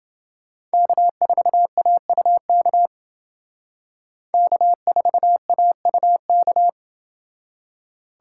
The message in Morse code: K4AUK K4AUK

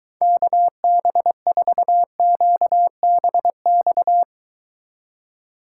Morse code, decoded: KB4QBX

